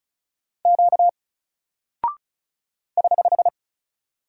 Morse code, heard as a keyed tone, about 36 wpm, 700 Hz. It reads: Q <HH>
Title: Single Character (Letter:Number:Punctuation:Prosign) Alt 35wpm